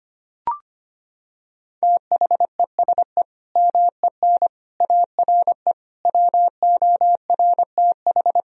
Morse code, heard as a keyed tone, about 25 wpm, 700 Hz. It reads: THESE MEN ARE WORTH
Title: Sentences from Top 1000 Words - Review Part 1 - 25wpm